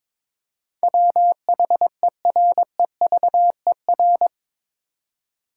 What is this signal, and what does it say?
Morse code, heard as a keyed tone, about 22 wpm, 700 Hz. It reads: WHEREVER
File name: practice / 22wpm / Eight Letter Words - 22wpm